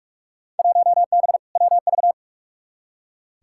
Morse code, heard as a keyed tone, about 46 wpm, 700 Hz. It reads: 1B WV
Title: Field Day Exchanges Encore - 45wpm